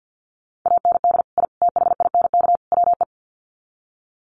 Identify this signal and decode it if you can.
Morse code, heard as a keyed tone, about 50 wpm, 700 Hz. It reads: AND I THINK WE
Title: Sets of 4 Words 50wpm